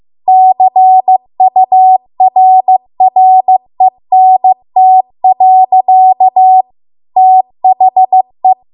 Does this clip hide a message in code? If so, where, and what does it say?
Yes: Morse code from 0.3 to 8.5 s, heard as a keyed tone, about 15 wpm, 750 Hz. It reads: CURRENT. THE